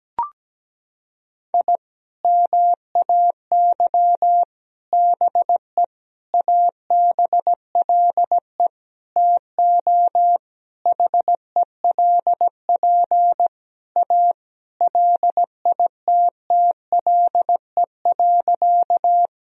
Morse code, heard as a keyed tone, about 17 wpm, 700 Hz. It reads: I MAY BE ABLE TO HELP A LITTLE.
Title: Sentences from Top 1000 Words - Review Part 2 - 17wpm